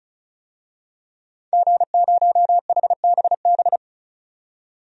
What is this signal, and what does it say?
Morse code, heard as a keyed tone, about 34 wpm, 700 Hz. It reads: G0HBB
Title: DXCC-100 Call Signs - 5 Characters - 35wpm